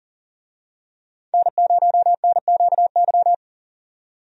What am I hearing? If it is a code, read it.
Morse code, heard as a keyed tone, about 40 wpm, 700 Hz. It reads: N0NQY
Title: USA Call Signs - 5 Characters - 40wpm